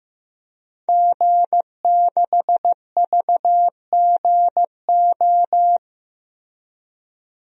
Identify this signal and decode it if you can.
Morse code, heard as a keyed tone, about 15 wpm, 700 Hz. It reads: G6VGO
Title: DXCC-100 Call Signs - 5 Characters - Repeated 2x - 15wpm